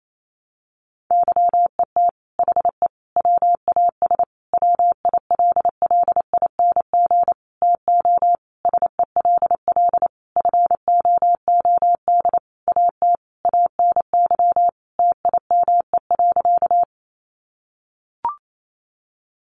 Morse code, heard as a keyed tone, about 28 wpm, 700 Hz. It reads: YET HE WAS WILLING TO SELL FOOD AT ANY TIME.
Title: Sentences from Top 1100 Words 28wpm